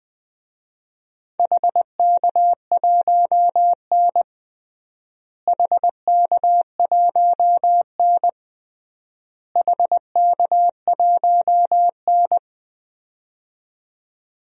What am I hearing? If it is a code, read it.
Morse code, heard as a keyed tone, about 21 wpm, 700 Hz. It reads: HK1N HK1N HK1N
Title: DXCC-100 Call Signs - 4 Characters - Repeated 3x - 20wpm